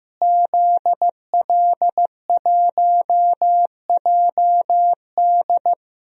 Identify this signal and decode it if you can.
Morse code, heard as a keyed tone, about 16 wpm, 700 Hz. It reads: ZL1JD
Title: DXCC-100 Call Signs - 5 Characters - Repeated 3x - 15wpm